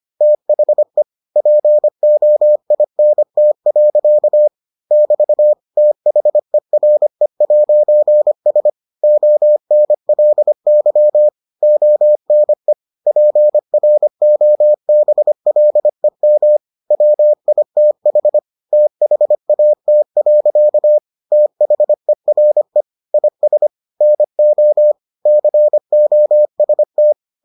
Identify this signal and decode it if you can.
Morse code, heard as a keyed tone, about 25 wpm, 600 Hz. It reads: THE POINT. <BT> THERE'S ONLY ONE PROBLEM WITH THAT. THERE IS NO COST